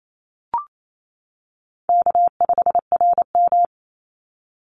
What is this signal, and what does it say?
Morse code, heard as a keyed tone, about 28 wpm, 700 Hz. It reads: K5RM
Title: Easy Call Signs - 28wpm